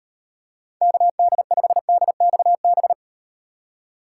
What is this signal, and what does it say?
Morse code, heard as a keyed tone, about 38 wpm, 700 Hz. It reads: KD5DXB